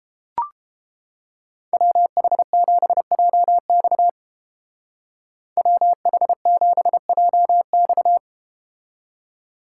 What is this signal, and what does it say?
Morse code, heard as a keyed tone, about 31 wpm, 700 Hz. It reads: WH7JX WH7JX